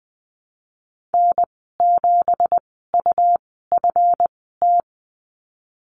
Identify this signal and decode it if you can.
Morse code, heard as a keyed tone, about 20 wpm, spaced as for 15 wpm, 700 Hz. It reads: N7UFT
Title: 167 Lesson 27g - Number 7 - Call Signs 20wpm